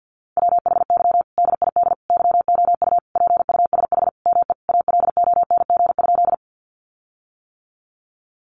Morse code, heard as a keyed tone, about 50 wpm, 700 Hz. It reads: WHY DID YOU PUSH ME ALONG?
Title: Sentences from Top 1300 Words 50wpm